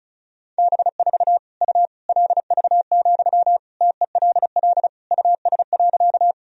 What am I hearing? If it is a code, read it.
Morse code, heard as a keyed tone, about 36 wpm, 700 Hz. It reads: B4 U LV, TELL US.
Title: Sentences from Top 400 Words and Abbreviations 35wpm